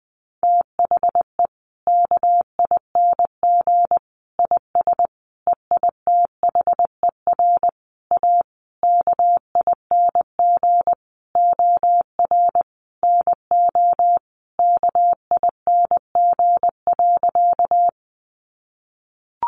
Morse code, heard as a keyed tone, about 20 wpm, 700 Hz. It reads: THE KING IS EITHER A KING OR NO KING.
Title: Sentences from Top 600 Words 20wpm